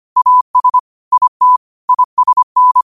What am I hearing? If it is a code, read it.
Morse code, heard as a keyed tone, about 25 wpm, 1000 Hz. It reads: AS IT ISN
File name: qotdfiles / 25wpm